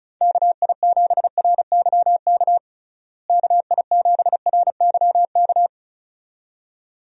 Morse code, heard as a keyed tone, about 36 wpm, 700 Hz. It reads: KI7RYK KI7RYK
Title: USA Call Signs - 6 Characters - Repeated 2x - 35wpm